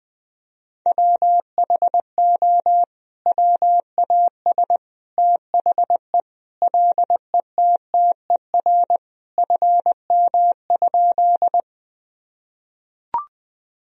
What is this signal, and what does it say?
Morse code, heard as a keyed tone, about 20 wpm, 700 Hz. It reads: WHO WAS THE LETTER FM?
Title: Sentences from Top 300 Words and Abbreviations 20wpm